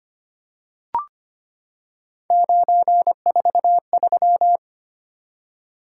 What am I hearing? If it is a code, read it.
Morse code, heard as a keyed tone, about 25 wpm, 700 Hz. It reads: 943
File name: Three Letter-Number Combinations - 25wpm